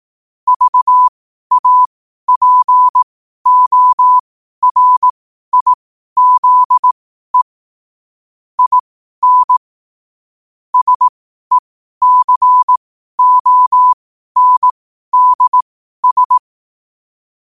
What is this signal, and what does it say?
Morse code, heard as a keyed tone, about 18 wpm, spaced as for 12 wpm, 1000 Hz. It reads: VAPORIZE IN SECONDS
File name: qotdfiles / 12wpm